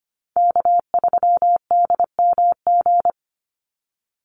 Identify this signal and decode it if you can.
Morse code, heard as a keyed tone, about 25 wpm, 700 Hz. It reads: K3DMG
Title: Call Signs Repeated 2x - 25wpm